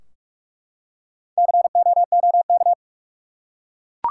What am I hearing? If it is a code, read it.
Morse code, heard as a keyed tone, about 43 wpm, 700 Hz. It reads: COOK
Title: Four Letter Words - 45wpm